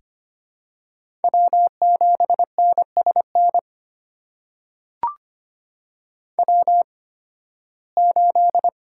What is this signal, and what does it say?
Morse code, heard as a keyed tone, about 25 wpm, spaced as for 16 wpm, 700 Hz. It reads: W7NSN W8
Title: USA Call Signs - 5 Characters - 8x Character Spacing - 25wpm